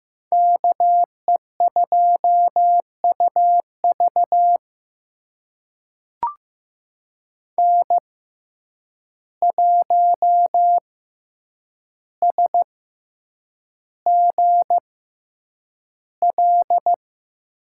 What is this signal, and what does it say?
Morse code, heard as a keyed tone, about 15 wpm, spaced as for 8 wpm, 700 Hz. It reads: KE2UV N1SGL
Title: USA Call Signs - 5 Characters - 6x Character Spacing - 15wpm